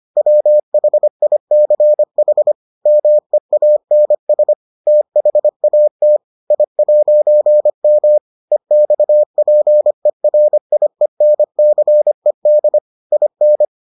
Morse code, heard as a keyed tone, about 25 wpm, 600 Hz. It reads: WHICH MEANS THAT I'M EXPERIENCED IN